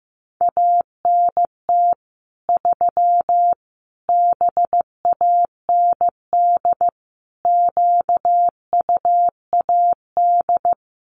Morse code, heard as a keyed tone, about 15 wpm, 700 Hz. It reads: ANT 3 BAND QUAD
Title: Common QSO Phrases 15wpm